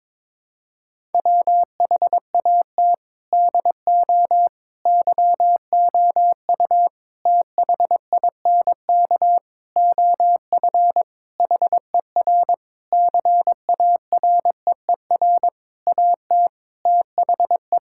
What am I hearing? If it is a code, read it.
Morse code, heard as a keyed tone, about 22 wpm, 700 Hz. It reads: WHAT DO YOU THINK OF HER CAREER AT THE
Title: Sentences from Top 1000 Words - Review Part 6 - 22wpm